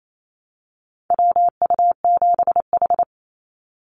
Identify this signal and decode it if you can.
Morse code, heard as a keyed tone, about 28 wpm, 700 Hz. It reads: WU7H
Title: NA SOTA Activator Callsigns 28wpm